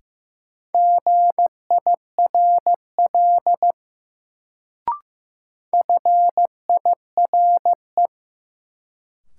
Morse code, heard as a keyed tone, about 15 wpm, 700 Hz. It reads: GIRL FIRE
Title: Sentences from Top 400 Words - New Word Drill - 15wpm